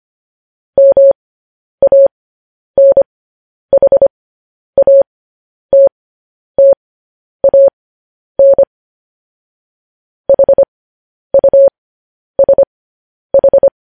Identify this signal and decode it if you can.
Morse code, heard as a keyed tone, about 25 wpm, spaced as for 10 wpm, 550 Hz. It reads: MANHATTAN HUSH
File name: podcast / News headlines-1x-25x10